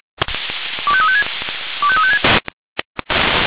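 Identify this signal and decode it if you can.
Five-tone selective call (ZVEI), heard as quick runs of five tones, tones of about 70 ms. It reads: 25356 25356
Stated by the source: Five-tone ZVEI selective call received on an SDR.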